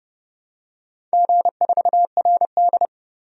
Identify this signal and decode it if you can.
Morse code, heard as a keyed tone, about 30 wpm, 700 Hz. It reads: G4RD
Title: DXCC-100 Call Signs - 4 Characters - 30wpm